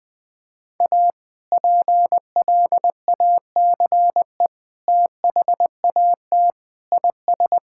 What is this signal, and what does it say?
Morse code, heard as a keyed tone, about 20 wpm, 700 Hz. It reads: A PLACE THAT IS
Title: Sentences with 2-Word Phrases - 20wpm